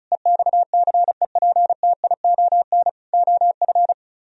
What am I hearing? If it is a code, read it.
Morse code, heard as a keyed tone, about 44 wpm, 700 Hz. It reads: EXCEPTION OF
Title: Speed-Racing - Sets of 4 Words 35wpm